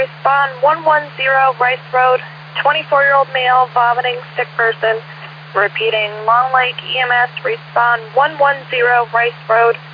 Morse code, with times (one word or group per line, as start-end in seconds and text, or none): none